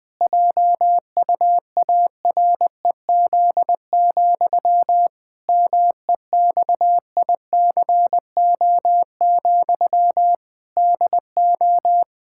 Morse code, heard as a keyed tone, about 20 wpm, 700 Hz. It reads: JUAREZ, MEXICO, DO
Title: Pangram Challenge v2 - 20wpm